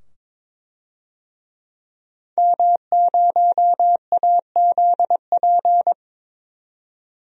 Morse code, heard as a keyed tone, about 22 wpm, 700 Hz. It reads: M0AZP
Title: DXCC-100 Call Signs - 5 Characters - 22wpm